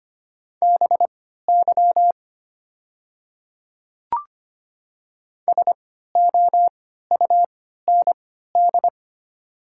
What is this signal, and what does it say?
Morse code, heard as a keyed tone, about 25 wpm, spaced as for 14 wpm, 700 Hz. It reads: BY SOUND
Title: 3x Character Spacing - Top 100 Words - 25wpm